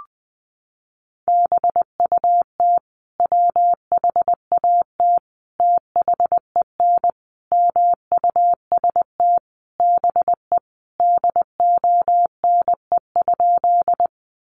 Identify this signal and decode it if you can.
Morse code, heard as a keyed tone, about 20 wpm, 700 Hz. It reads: BUT WHAT THEN MUST BE DONE?
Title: Sentences from Top 500 Words 20wpm